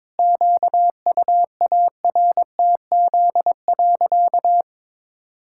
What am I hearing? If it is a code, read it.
Morse code, heard as a keyed tone, about 22 wpm, 700 Hz. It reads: QUARTZ.
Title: Pangram Challenge v2 - 22wpm